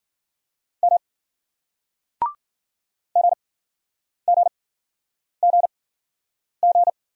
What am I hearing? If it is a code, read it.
Morse code, heard as a keyed tone, about 56 wpm, 700 Hz. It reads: M G G G G